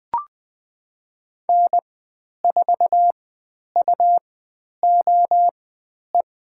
Morse code, heard as a keyed tone, about 20 wpm, spaced as for 10 wpm, 700 Hz. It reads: N4UOE
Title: 102 Lesson 17g - Letter u - Call Signs 20wpm